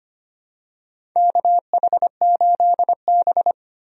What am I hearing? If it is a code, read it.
Morse code, heard as a keyed tone, about 25 wpm, 700 Hz. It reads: KH8B
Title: USA Call Signs - 4 Characters - Repeated 2x - 25wpm